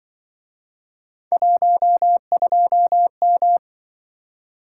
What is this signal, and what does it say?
Morse code, heard as a keyed tone, about 24 wpm, 700 Hz. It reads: 12M
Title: Speed-Racing - Three Letter-Number Combinations - 20wpm